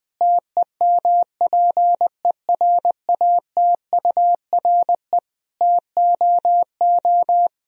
Morse code, heard as a keyed tone, about 20 wpm, 700 Hz. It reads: TEMPERATURE TOO
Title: Sentences from Top 1000 Words - Review Part 4 - 20wpm